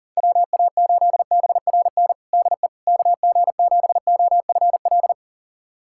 Morse code, heard as a keyed tone, about 40 wpm, 700 Hz. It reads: WA8BRN DE KG7OFL
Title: USA Callsign DE Callsign - 6 Characters - 40wpm